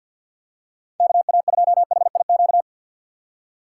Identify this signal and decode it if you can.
Morse code, heard as a keyed tone, about 47 wpm, 700 Hz. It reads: KA2HIQ
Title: USA Call Signs - 6 Characters - 4x Character Spacing - 50wpm